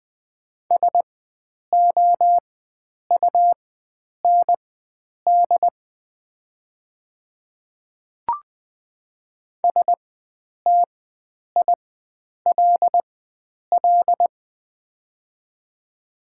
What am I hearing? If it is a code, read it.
Morse code, heard as a keyed tone, about 21 wpm, spaced as for 9 wpm, 700 Hz. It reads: SOUND STILL
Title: 4x Character Spacing - Top 500 Words - 20wpm